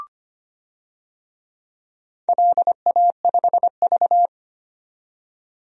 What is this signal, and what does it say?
Morse code, heard as a keyed tone, about 25 wpm, 700 Hz. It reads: LA5V
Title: DXCC-100 Call Signs - 4 Characters - 25wpm